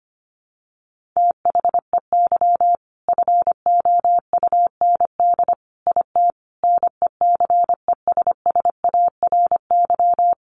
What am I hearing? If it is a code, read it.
Morse code, heard as a keyed tone, about 25 wpm, 700 Hz. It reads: THEY FOUND IT NECESSARY
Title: Sentences from Top 1400 Words 25wpm